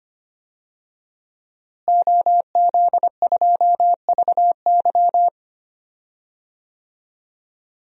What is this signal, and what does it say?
Morse code, heard as a keyed tone, about 25 wpm, 700 Hz. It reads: OZ2VY
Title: DXCC-100 Call Signs - 5 Characters - Repeated 3x - 25wpm